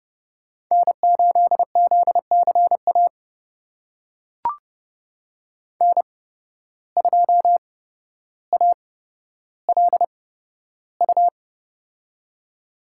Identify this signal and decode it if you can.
Morse code, heard as a keyed tone, about 31 wpm, spaced as for 13 wpm, 700 Hz. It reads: N8ZCA N2ALU
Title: USA Call Signs - 5 Characters - 8x Character Spacing - 30wpm